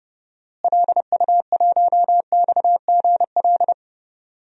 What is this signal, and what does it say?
Morse code, heard as a keyed tone, about 30 wpm, 700 Hz. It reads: LU1XGL
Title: DXCC-100 Call Signs - 6 Characters - Repeated 3x - 30wpm